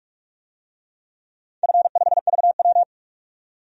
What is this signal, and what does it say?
Morse code, heard as a keyed tone, about 44 wpm, 700 Hz. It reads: F5VW
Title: DXCC-100 Call Signs - 4 Characters - 45wpm